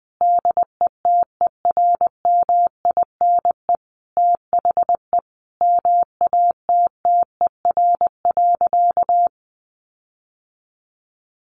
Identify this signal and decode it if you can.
Morse code, heard as a keyed tone, about 20 wpm, 700 Hz. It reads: DETERMINE THE MATTER.
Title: Sentences from Top 1300 Words 20wpm